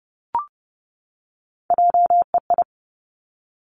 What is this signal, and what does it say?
Morse code, heard as a keyed tone, about 30 wpm, 700 Hz. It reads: JEI
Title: Three Challenging Letter-Number Combinations 30wpm